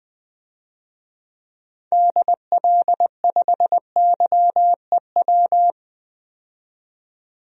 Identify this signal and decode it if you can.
Morse code, heard as a keyed tone, about 20 wpm, 700 Hz. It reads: DL5YEW
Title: DXCC-100 Call Signs - 6 Characters - 20wpm